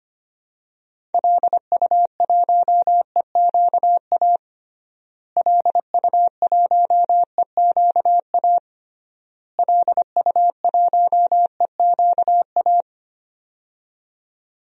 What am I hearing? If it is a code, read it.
Morse code, heard as a keyed tone, about 25 wpm, 700 Hz. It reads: LU1EQA LU1EQA LU1EQA
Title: DXCC-100 Call Signs - 6 Characters - Repeated 3x - 25wpm